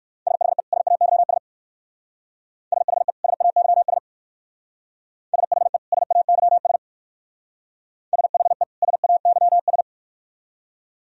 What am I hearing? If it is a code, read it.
Morse code, heard as a keyed tone, about 55 wpm, 700 Hz. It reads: SHE SAYS SHE SAYS SHE SAYS SHE SAYS
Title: Speed-Racing - Sets of 2 Words 45wpm